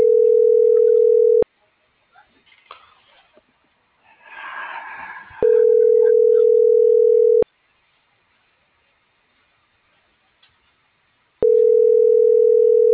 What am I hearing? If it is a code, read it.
Ringback tone (North America), 440 + 480 Hz.